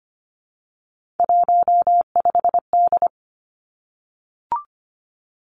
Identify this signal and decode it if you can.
Morse code, heard as a keyed tone, about 25 wpm, 700 Hz. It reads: 15D
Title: Three Challenging Letter-Number Combinations 25wpm